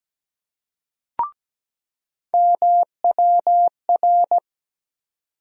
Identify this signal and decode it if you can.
Morse code, heard as a keyed tone, about 17 wpm, 700 Hz. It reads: MWR
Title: USA Call Sign Postfixes - 3 Letters 17wpm